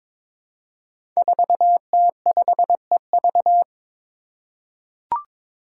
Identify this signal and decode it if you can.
Morse code, heard as a keyed tone, about 22 wpm, 700 Hz. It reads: 4T5EV
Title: Five Challenging Letter-Number Combinations 22wpm